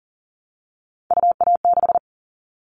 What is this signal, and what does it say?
Morse code, heard as a keyed tone, about 40 wpm, 700 Hz. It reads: UA6
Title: Callsign Prefixes for Top 100 DXCC - 40wpm